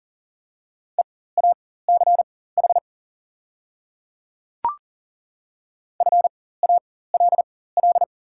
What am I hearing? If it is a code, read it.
Morse code, heard as a keyed tone, about 40 wpm, spaced as for 19 wpm, 700 Hz. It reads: EACH FALL
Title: 4x Character Spacing - Top 500 Words - 40wpm